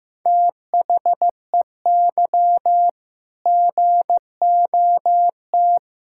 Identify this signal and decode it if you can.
Morse code, heard as a keyed tone, about 15 wpm, 700 Hz. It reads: THEY GOT
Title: Sentences from Top 1600 Words 15wpm